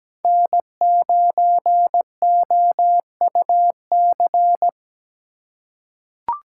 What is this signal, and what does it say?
Morse code, heard as a keyed tone, about 17 wpm, 700 Hz. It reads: N9OUC
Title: USA Call Signs - 5 Characters - 3x Character Spacing - 17wpm